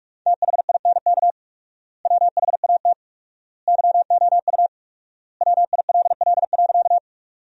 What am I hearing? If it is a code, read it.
Morse code, heard as a keyed tone, about 46 wpm, 700 Hz. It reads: THINK WHAT YOU WILL.
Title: 4x Word-Spacing - Sentences from Top 200 Words - 45wpm